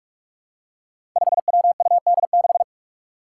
Morse code, heard as a keyed tone, about 46 wpm, 700 Hz. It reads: 5WUD6
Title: Five Challenging Letter-Number Combinations 45wpm